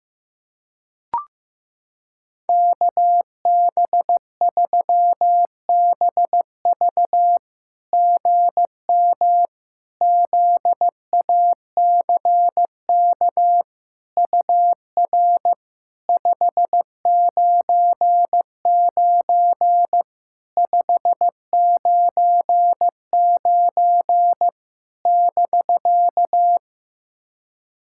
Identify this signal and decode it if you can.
Morse code, heard as a keyed tone, about 15 wpm, 700 Hz. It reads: KB3BV GM ZACK UR 599 599 <BK>
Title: SOTA QSO Practice 15wpm